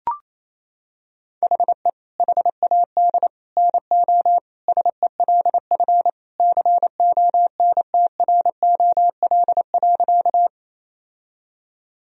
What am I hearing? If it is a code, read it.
Morse code, heard as a keyed tone, about 27 wpm, 700 Hz. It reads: HE HAD NO SELF CONTROL.
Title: Sentences from Top 1000 Words - Review Part 3 - 28wpm